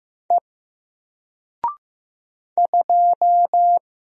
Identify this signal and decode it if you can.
Morse code, heard as a keyed tone, about 16 wpm, 700 Hz. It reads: E 2
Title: Single Letter-Number Alt 15wpm